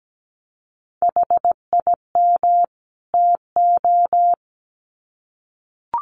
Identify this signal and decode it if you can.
Morse code, heard as a keyed tone, about 17 wpm, 700 Hz. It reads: HIM TO
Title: Sets of 2 Words 17wpm